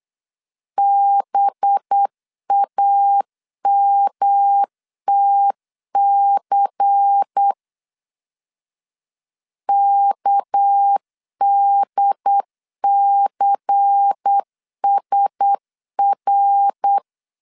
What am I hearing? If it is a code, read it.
Morse code, heard as a keyed tone, about 8 wpm, 800 Hz. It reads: BAMTC KDCSR
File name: LEZIONI / 6 PKX0 / 6 PK 8wpm 3003